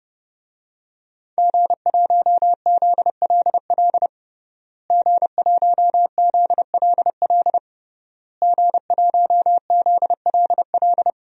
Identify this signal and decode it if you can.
Morse code, heard as a keyed tone, about 30 wpm, 700 Hz. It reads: G1ZLL G1ZLL G1ZLL
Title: DXCC-100 Call Signs - 5 Characters - Repeated 3x - 30wpm